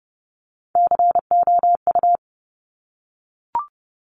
Morse code, heard as a keyed tone, about 30 wpm, 700 Hz. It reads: COU
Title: Top Three Letter Combinations in Words 30wpm